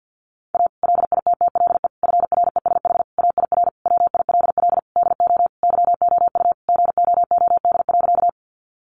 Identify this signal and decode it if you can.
Morse code, heard as a keyed tone, about 50 wpm, 700 Hz. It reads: A LITTLE FRESH AIR WILL DO YOU GOOD.
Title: Sentences from Top 1200 Words 50wpm